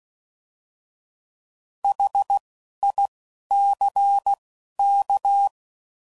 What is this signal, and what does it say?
Morse code, heard as a keyed tone, about 16 wpm, spaced as for 11 wpm, 800 Hz. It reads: HICK